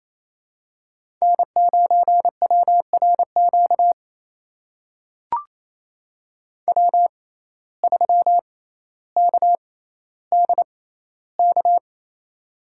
Morse code, heard as a keyed tone, about 28 wpm, spaced as for 14 wpm, 700 Hz. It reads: N9WRQ W3KDK
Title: USA Call Signs - 5 Characters - 6x Character Spacing - 28wpm